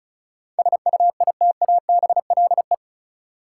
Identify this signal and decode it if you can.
Morse code, heard as a keyed tone, about 36 wpm, 700 Hz. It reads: SUITABLE
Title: Sentences from Top 1400 Words - New Word Drill - 35wpm